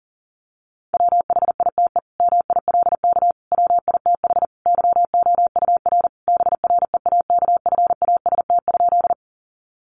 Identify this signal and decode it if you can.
Morse code, heard as a keyed tone, about 37 wpm, 700 Hz. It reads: WHITE MILK WITH YOUR BREAKFAST?
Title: Sentences from Top 1500 Words 40wpm